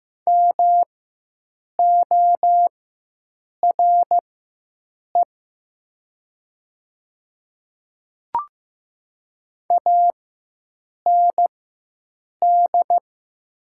Morse code, heard as a keyed tone, about 15 wpm, spaced as for 7 wpm, 690 Hz. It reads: MORE AND
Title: 4x Character Spacing - Top 200 Words - 15wpm